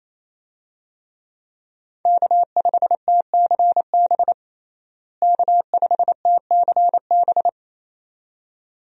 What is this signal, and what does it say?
Morse code, heard as a keyed tone, about 27 wpm, 700 Hz. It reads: K5TCB K5TCB
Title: USA Call Signs - 5 Characters - Repeated 2x - 28wpm